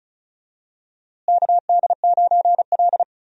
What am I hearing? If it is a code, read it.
Morse code, heard as a keyed tone, about 34 wpm, 700 Hz. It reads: KD9L
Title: USA Call Signs - 4 Characters - Repeated 3x - 35wpm